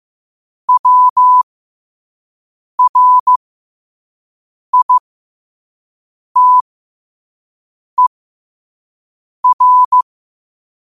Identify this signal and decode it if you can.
Morse code, heard as a keyed tone, about 15 wpm, spaced as for 5 wpm, 1000 Hz. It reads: WRITER
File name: qotdfiles / 5wpm